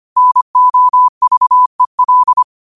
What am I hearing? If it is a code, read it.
Morse code, heard as a keyed tone, about 25 wpm, 1000 Hz. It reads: NOVEL